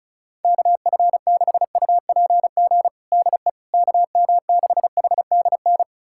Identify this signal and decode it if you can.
Morse code, heard as a keyed tone, about 36 wpm, 700 Hz. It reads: KF6UPG DE KM6HDN
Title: USA Callsign DE Callsign - 6 Characters - 35wpm